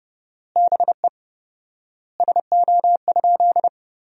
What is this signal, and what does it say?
Morse code, heard as a keyed tone, about 32 wpm, 700 Hz. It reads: BE SO?
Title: 4x Word-Spacing - Sentences from Top 300 Words - 30wpm